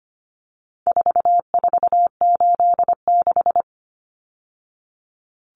Morse code, heard as a keyed tone, about 25 wpm, 700 Hz. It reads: 4486
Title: Numbers - 4-Digit 25wpm